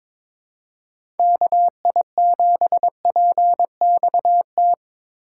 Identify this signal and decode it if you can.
Morse code, heard as a keyed tone, about 22 wpm, 700 Hz. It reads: KI7PXT
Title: USA Call Signs - 6 Characters - 2x Character Spacing - 22wpm